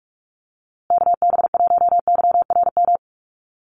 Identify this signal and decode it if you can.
Morse code, heard as a keyed tone, about 44 wpm, 700 Hz. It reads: KB1YRM
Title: USA Call Signs Repeated 2x 45wpm